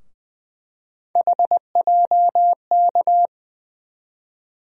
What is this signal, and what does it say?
Morse code, heard as a keyed tone, about 21 wpm, 700 Hz. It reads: HJK
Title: USA Call Sign Postfixes - 3 Letters 20wpm